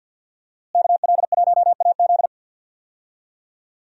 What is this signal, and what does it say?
Morse code, heard as a keyed tone, about 46 wpm, 700 Hz. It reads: KL1AZ
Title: USA Call Signs - 5 Characters - Repeated 3x - 50wpm